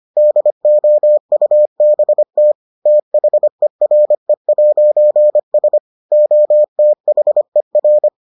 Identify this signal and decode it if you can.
Morse code, heard as a keyed tone, about 25 wpm, 600 Hz. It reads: DOUBT THERE'S OTHER